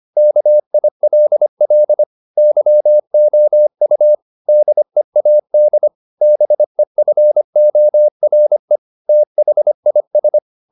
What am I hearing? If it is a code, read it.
Morse code, heard as a keyed tone, about 25 wpm, 600 Hz. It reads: KILL YOU DEAD BEFORE THIS
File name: the old man and the sea part 1 25wpm